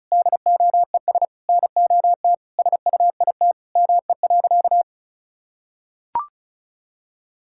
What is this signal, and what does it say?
Morse code, heard as a keyed tone, about 34 wpm, 700 Hz. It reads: DOES NOT SUIT ME.
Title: Sentences from Top 1600 Words 35wpm